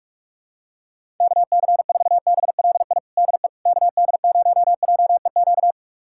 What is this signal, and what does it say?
Morse code, heard as a keyed tone, about 46 wpm, 700 Hz. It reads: KC4BLI DE KD0JEQ
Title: USA Callsign DE Callsign 45wpm